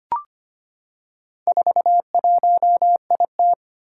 Morse code, heard as a keyed tone, about 25 wpm, 700 Hz. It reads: 41IT
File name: practice / 25wpm / Four Letter-Number Combinations - 25wpm